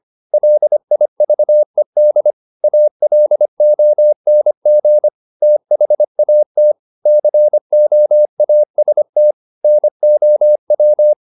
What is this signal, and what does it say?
Morse code, heard as a keyed tone, about 25 wpm, 600 Hz. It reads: LIVED ALONG THAT COAST NOW
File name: the old man and the sea part 1 25wpm